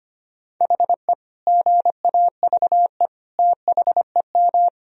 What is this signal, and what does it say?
Morse code, heard as a keyed tone, about 25 wpm, 700 Hz. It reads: HE GAVE THEM
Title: Sentences from Top 1000 Words - Review Part 3 - 25wpm